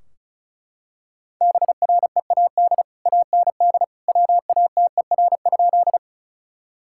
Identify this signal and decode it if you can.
Morse code, heard as a keyed tone, about 34 wpm, 700 Hz. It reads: BREAD AND WATER?
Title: Sentences from Top 1600 Words 35wpm